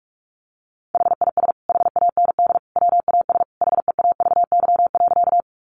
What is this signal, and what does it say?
Morse code, heard as a keyed tone, about 44 wpm, 700 Hz. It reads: HIS HAND WAS HEAVY.
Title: Sentences from Top 1100 Words 45wpm